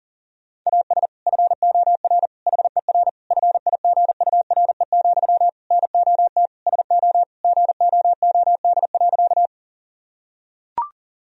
Morse code, heard as a keyed tone, about 37 wpm, 700 Hz. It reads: AS FOR HER FIGURE, NOT SO GOOD.
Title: Sentences from Top 1000 Words - Review Part 2 - 40wpm